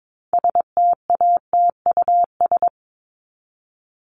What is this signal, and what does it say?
Morse code, heard as a keyed tone, about 22 wpm, 700 Hz. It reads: STATUS
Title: Top 1251-1500 Words 22wpm